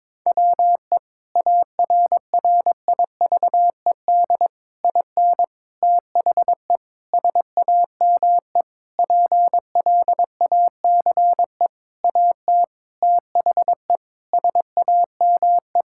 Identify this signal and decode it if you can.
Morse code, heard as a keyed tone, about 22 wpm, 700 Hz. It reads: WE ARRIVED IN THE SAME PLACE AT THE SAME
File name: practice / 22wpm / Sentences with 8-Word Phrases - 22wpm